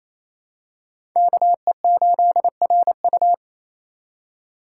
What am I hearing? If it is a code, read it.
Morse code, heard as a keyed tone, about 27 wpm, 700 Hz. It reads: KE8RU
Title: K1USN SST QSO Practice - 28wpm